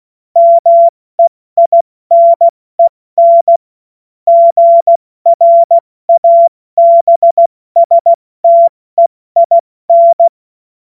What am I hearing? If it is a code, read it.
Morse code, heard as a keyed tone, about 16 wpm, 680 Hz. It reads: MEINEN GRABSTEIN
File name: Morsekurs / Witze / Witz 15. Juni 2021, 70 F80